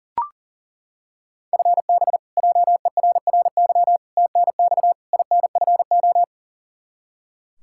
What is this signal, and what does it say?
Morse code, heard as a keyed tone, about 40 wpm, 700 Hz. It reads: FB JERRY TNX INFO
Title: K1USN SST QSO Practice 40wpm